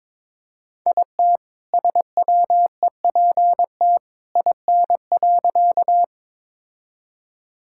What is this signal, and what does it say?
Morse code, heard as a keyed tone, about 22 wpm, 700 Hz. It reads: IT SWEPT IN.
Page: morse-code-ninja-practice-22wpm